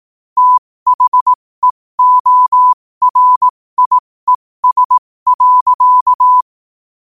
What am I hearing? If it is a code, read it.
Morse code, heard as a keyed tone, about 18 wpm, spaced as for 15 wpm, 1000 Hz. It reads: THEORIES.
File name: qotdfiles / 15wpm